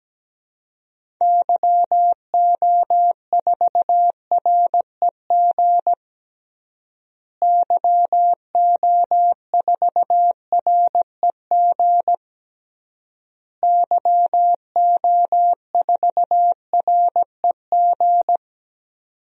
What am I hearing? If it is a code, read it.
Morse code, heard as a keyed tone, about 17 wpm, 700 Hz. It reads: YO4REG YO4REG YO4REG
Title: DXCC-100 Call Signs - 6 Characters - Repeated 3x - 17wpm